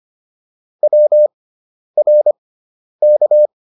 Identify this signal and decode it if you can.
Morse code, heard as a keyed tone, about 25 wpm, spaced as for 10 wpm, 600 Hz. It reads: WRK